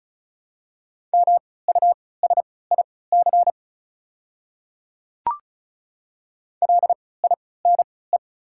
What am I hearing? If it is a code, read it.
Morse code, heard as a keyed tone, about 36 wpm, spaced as for 20 wpm, 690 Hz. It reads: MUSIC LINE
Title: 3x Character Spacing - Top 300 Words - 35wpm